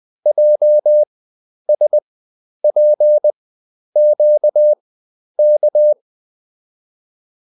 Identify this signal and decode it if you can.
Morse code, heard as a keyed tone, about 20 wpm, spaced as for 10 wpm, 600 Hz. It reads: JSPQK